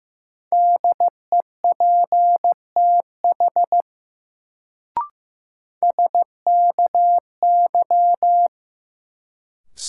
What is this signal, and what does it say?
Morse code, heard as a keyed tone, about 15 wpm, 700 Hz. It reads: DEPTH SKY
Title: Sentences from Top 1200 Words - New Word Drill Alt - 15wpm